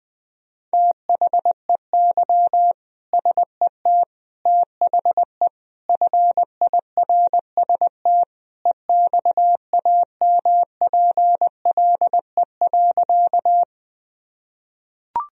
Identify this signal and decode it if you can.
Morse code, heard as a keyed tone, about 20 wpm, 700 Hz. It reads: THEY SET THE FIRST EXAMPLE.
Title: Sentences from Top 1000 Words - Review Part 1 - 20wpm